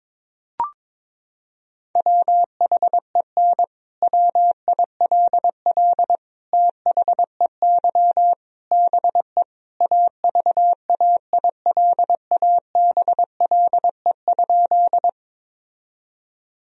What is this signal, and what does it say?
Morse code, heard as a keyed tone, about 22 wpm, 700 Hz. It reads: WHEN WILL THEY BE AVAILABLE?
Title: Sentences from Top 1000 Words - Review Part 3 - 22wpm